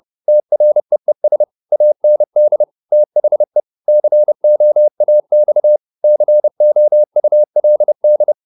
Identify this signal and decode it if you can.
Morse code, heard as a keyed tone, about 30 wpm, 600 Hz. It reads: TREES AND THE COAX COULD